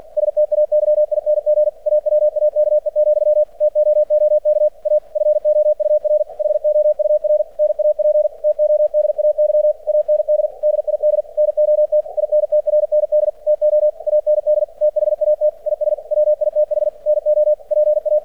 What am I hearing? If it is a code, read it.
Morse code, heard as a keyed tone, about 48 wpm, 600 Hz. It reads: STAYING AWAKE, TOOK A FOUR HOUR NAP TODAY AND DID NOT INTRND TO AND THAT IS WITH NO PA
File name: archived / CRSnet